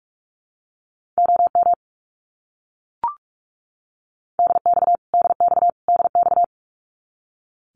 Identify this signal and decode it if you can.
Morse code, heard as a keyed tone, about 44 wpm, 700 Hz. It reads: OM DX DX DX
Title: Abbreviations Repeated 3x - 45wpm